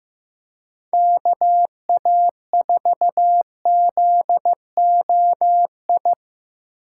Morse code, heard as a keyed tone, about 18 wpm, 700 Hz. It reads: KA4ZOI
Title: Speed Racing - USA 2x3 Call Signs 15wpm